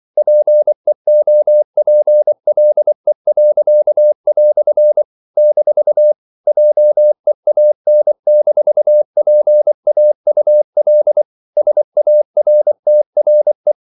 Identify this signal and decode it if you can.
Morse code, heard as a keyed tone, about 24 wpm, 600 Hz. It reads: PEOPLE." - JEAN-PAUL SARTRE
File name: CW / Practice / EnglishQuotations0324